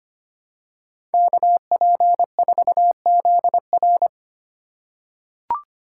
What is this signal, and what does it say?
Morse code, heard as a keyed tone, about 25 wpm, 700 Hz. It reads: KP4ZR
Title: USA Call Signs - 5 Characters - 2x Character Spacing - 25wpm